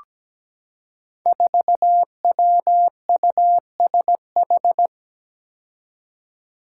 Morse code, heard as a keyed tone, about 17 wpm, 700 Hz. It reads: 4WUSH
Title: Five Challenging Letter-Number Combinations 17wpm